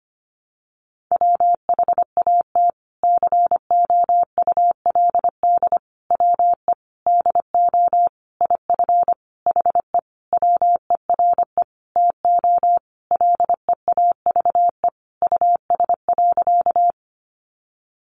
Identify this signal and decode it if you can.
Morse code, heard as a keyed tone, about 25 wpm, 700 Hz. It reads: WHAT COULD WE DO IF HE WERE TO LEAVE US.
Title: Sentences from Top 400 Words 25wpm